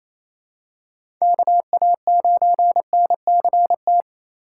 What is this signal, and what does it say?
Morse code, heard as a keyed tone, about 28 wpm, 700 Hz. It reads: KA9NCT
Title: USA Call Signs - 6 Characters - 2x Character Spacing - 28wpm